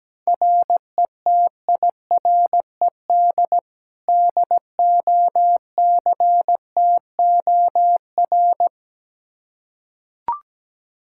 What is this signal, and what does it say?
Morse code, heard as a keyed tone, about 17 wpm, 700 Hz. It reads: RETIRED DOCTOR
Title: Practice QSO - Exchange 3 - More Info - 17wpm